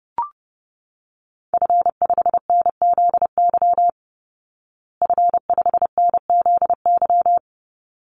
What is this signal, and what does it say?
Morse code, heard as a keyed tone, about 30 wpm, 700 Hz. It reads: F5NZY F5NZY
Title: Call Signs Repeated 3x - 30wpm